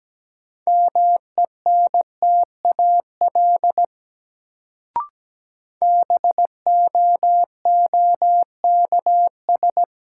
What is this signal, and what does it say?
Morse code, heard as a keyed tone, about 17 wpm, 700 Hz. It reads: MENTAL BOOKS
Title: Sentences from Top 1000 Words - Review Part 4 - New Word Drill - 17wpm